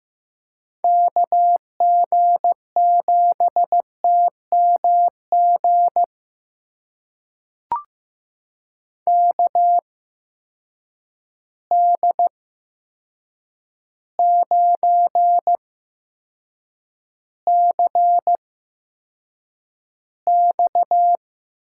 Morse code, heard as a keyed tone, about 15 wpm, spaced as for 7 wpm, 700 Hz. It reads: KG7TMG KD9CX
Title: USA Call Signs - 8x Character Spacing - 15wpm